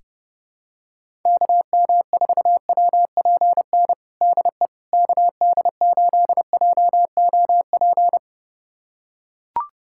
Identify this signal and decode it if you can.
Morse code, heard as a keyed tone, about 30 wpm, 700 Hz. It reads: KM4WPN DE KD8JOP
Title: USA Callsign DE Callsign - 6 Characters - 30wpm